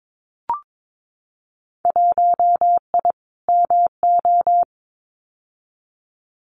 Morse code, heard as a keyed tone, about 22 wpm, 700 Hz. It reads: 1I MO
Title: Winter Field Day Exchanges Encore 22wpm